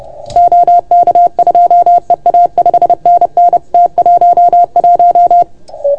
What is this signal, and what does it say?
Morse code, heard as a keyed tone, about 30 wpm, 690 Hz. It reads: OK2EA 5NN T11
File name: Archiv / CUC381 / OK1FGD CUC 381